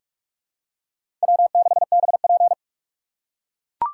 Morse code, heard as a keyed tone, about 44 wpm, 700 Hz. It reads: W6BP
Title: USA Call Signs - 4 Characters - 6x Character Spacing - 45wpm